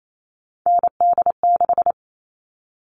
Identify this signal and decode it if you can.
Morse code, heard as a keyed tone, about 28 wpm, 700 Hz. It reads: ND6
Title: All Cut Numbers - 3-Digit 28wpm